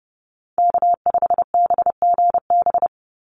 Morse code, heard as a keyed tone, about 30 wpm, 700 Hz. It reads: K5BGB
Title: Call Signs Repeated 3x - 30wpm